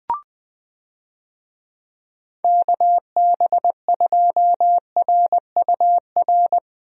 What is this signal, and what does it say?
Morse code, heard as a keyed tone, about 20 wpm, 700 Hz. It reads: KB2RUR
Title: USA Call Signs - 6 Characters - 20wpm